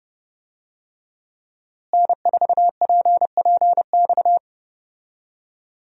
Morse code, heard as a keyed tone, about 30 wpm, 700 Hz. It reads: N4PPX
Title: USA Call Signs - 5 Characters - Repeated 2x - 30wpm